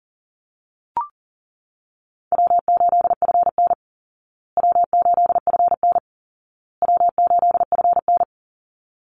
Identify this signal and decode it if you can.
Morse code, heard as a keyed tone, about 40 wpm, 700 Hz. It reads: W8FN W8FN W8FN
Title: Easy Call Signs Repeated 3x - 40wpm